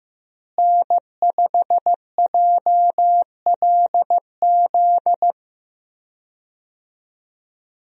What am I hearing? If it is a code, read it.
Morse code, heard as a keyed tone, about 15 wpm, 700 Hz. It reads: N5JLZ
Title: USA Call Signs - 5 Characters - Repeated 2x - 15wpm